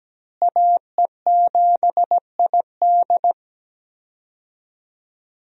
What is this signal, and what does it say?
Morse code, heard as a keyed tone, about 17 wpm, 700 Hz. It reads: AE7ID
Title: USA Call Signs - 5 Characters - Repeated 3x - 17wpm